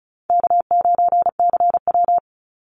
Morse code, heard as a keyed tone, about 34 wpm, 700 Hz. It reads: K9CW
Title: Easy Call Signs Repeated 3x - 35wpm